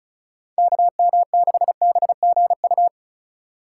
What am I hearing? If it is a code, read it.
Morse code, heard as a keyed tone, about 36 wpm, 700 Hz. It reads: KM6BGU
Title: USA Call Signs - 6 Characters - 3x Character Spacing - 35wpm